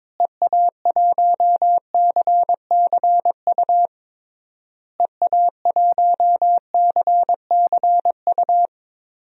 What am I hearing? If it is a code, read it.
Morse code, heard as a keyed tone, about 22 wpm, 700 Hz. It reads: EA1CCU EA1CCU
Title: DXCC-100 Call Signs - 6 Characters - Repeated 3x - 22wpm